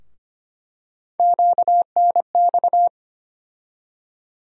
Morse code, heard as a keyed tone, about 25 wpm, 700 Hz. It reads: QNX
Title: USA Call Sign Postfixes - 3 Letters 25wpm